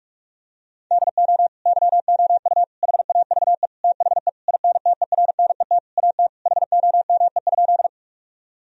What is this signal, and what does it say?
Morse code, heard as a keyed tone, about 46 wpm, 700 Hz. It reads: DO YOU HAVE THE INTERNET AT HOME?
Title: Sentences from Top 1000 Words - Review Part 4 - 45wpm